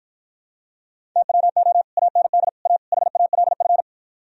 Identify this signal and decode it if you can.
Morse code, heard as a keyed tone, about 54 wpm, 700 Hz. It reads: TWO AND A HALF